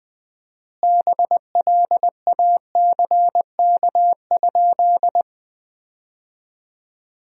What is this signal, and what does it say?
Morse code, heard as a keyed tone, about 20 wpm, 700 Hz. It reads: BLACK?
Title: 4x Word-Spacing - Sentences from Top 400 Words - 20wpm